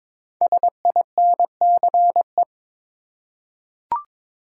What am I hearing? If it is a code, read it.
Morse code, heard as a keyed tone, about 22 wpm, 700 Hz. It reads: SINCE
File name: 5 Letter Words - First 3 Letters Given - 22wpm